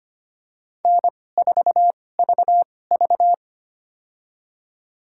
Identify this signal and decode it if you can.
Morse code, heard as a keyed tone, about 25 wpm, spaced as for 18 wpm, 700 Hz. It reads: N4VV
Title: USA Call Signs - 4 Characters - 2x Character Spacing - 25wpm